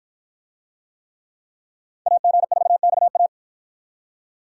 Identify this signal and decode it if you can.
Morse code, heard as a keyed tone, about 48 wpm, 700 Hz. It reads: AG4XA